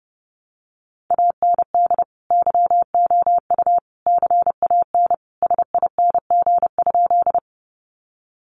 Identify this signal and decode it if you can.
Morse code, heard as a keyed tone, about 30 wpm, 700 Hz. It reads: AND YOU CAN SING?
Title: Sentences from Top 400 Words 30wpm